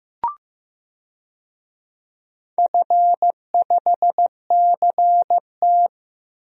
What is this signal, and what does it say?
Morse code, heard as a keyed tone, about 15 wpm, 700 Hz. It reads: F5CT
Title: DXCC-100 Call Signs - 4 Characters - Repeated 2x - 15wpm